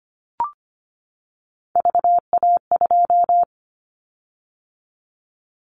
Morse code, heard as a keyed tone, about 25 wpm, 700 Hz. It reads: VA2
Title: Callsign Prefixes for Top 50 DXCC - 25wpm